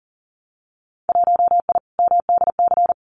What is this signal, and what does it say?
Morse code, heard as a keyed tone, about 42 wpm, 700 Hz. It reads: 1I MDC
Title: Winter Field Day Exchanges 40wpm